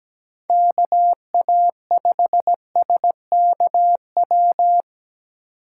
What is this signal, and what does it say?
Morse code, heard as a keyed tone, about 17 wpm, 700 Hz. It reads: KA5SKW
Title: USA Call Signs - 6 Characters - 3x Character Spacing - 17wpm